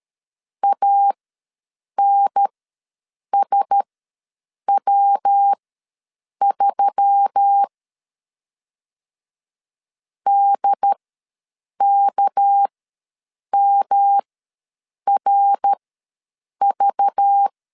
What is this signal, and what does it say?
Morse code, heard as a keyed tone, about 13 wpm, spaced as for 7 wpm, 800 Hz. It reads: ANSW3 DKMRV
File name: LEZIONI / 6 PKX0 / 6 PK ws 12wpm 3116